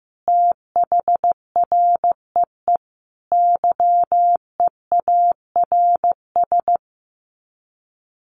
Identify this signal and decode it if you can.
Morse code, heard as a keyed tone, about 15 wpm, 700 Hz. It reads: THREE YEARS
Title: Sets of 5 Words Encore2 15wpm